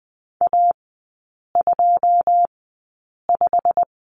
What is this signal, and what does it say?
Morse code, heard as a keyed tone, about 20 wpm, spaced as for 8 wpm, 700 Hz. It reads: A25
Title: 127 Lesson 21e - Number 6 - Three Characters 20wpm